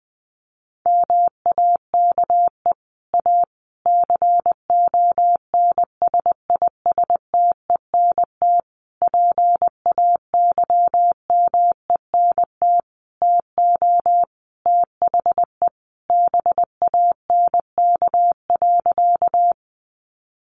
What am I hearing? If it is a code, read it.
Morse code, heard as a keyed tone, about 20 wpm, 700 Hz. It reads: MAKE A CONSISTENT PAYMENT TO THE BANK.
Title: Sentences from Top 1500 Words 20wpm